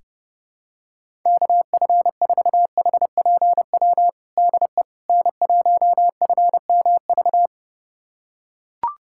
Morse code, heard as a keyed tone, about 32 wpm, 700 Hz. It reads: KF4HPW DE N1FMV
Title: USA Callsign DE Callsign 30wpm